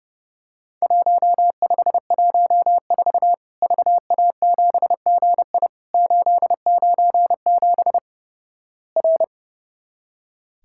Morse code, heard as a keyed tone, about 30 wpm, 700 Hz. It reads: 1514 VA7GI 897 R
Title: Worked All Europe DX Contest - QTC Exchange - 30wpm